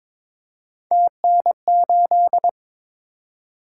Morse code, heard as a keyed tone, about 22 wpm, 700 Hz. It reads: TN8
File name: practice / 22wpm / Three Letter-Number Combinations - 22wpm